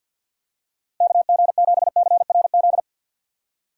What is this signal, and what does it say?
Morse code, heard as a keyed tone, about 52 wpm, 700 Hz. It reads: KG7CRZ
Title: USA Call Signs - 4x Character Spacing - 50wpm